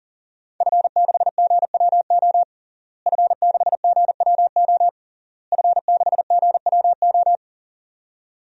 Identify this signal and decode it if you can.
Morse code, heard as a keyed tone, about 40 wpm, 700 Hz. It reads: F6GWO F6GWO F6GWO
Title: DXCC-100 Call Signs - 5 Characters - Repeated 3x - 40wpm